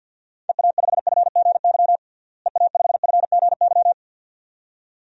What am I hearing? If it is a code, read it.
Morse code, heard as a keyed tone, about 53 wpm, 700 Hz. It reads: EA5FGY EA5FGY
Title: DXCC-100 Call Signs - 6 Characters - Repeated 2x - 50wpm